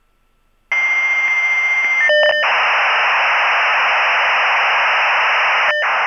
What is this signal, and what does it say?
Dial-up modem handshake.